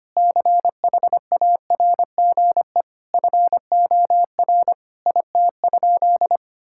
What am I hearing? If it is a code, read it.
Morse code, heard as a keyed tone, about 25 wpm, 700 Hz. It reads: CHARGE FOR IT?
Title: Sentences from Top 1000 Words - Review Part 6 - 25wpm